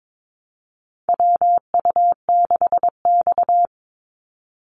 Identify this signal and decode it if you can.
Morse code, heard as a keyed tone, about 22 wpm, 700 Hz. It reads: WU6X
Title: Call Signs - 22wpm